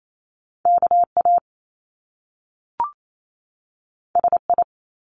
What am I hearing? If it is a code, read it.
Morse code, heard as a keyed tone, about 28 wpm, 700 Hz. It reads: KA SI
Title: Top Two Letter Combinations in Words 28wpm